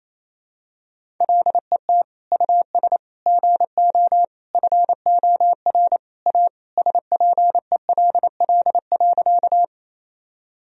Morse code, heard as a keyed tone, about 28 wpm, 700 Hz. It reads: LET US GO FOR A SPELL.
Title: Sentences from Top 1000 Words - Review Part 1 - 28wpm